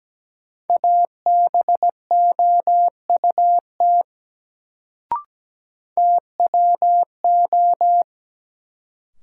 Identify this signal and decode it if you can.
Morse code, heard as a keyed tone, about 17 wpm, 700 Hz. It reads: ABOUT TWO
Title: Sentences from Top 100 Words - New Word Drill - 17wpm